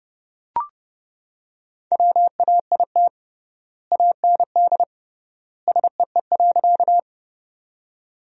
Morse code, heard as a keyed tone, about 30 wpm, 700 Hz. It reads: WAIT AND SEE.
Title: 3x Word-Spacing - Sentences from Top 500 Words - 30wpm